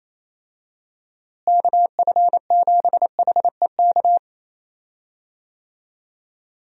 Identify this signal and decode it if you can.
Morse code, heard as a keyed tone, about 28 wpm, 700 Hz. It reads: KF7HEK
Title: USA Call Signs - 6 Characters - Repeated 3x - 28wpm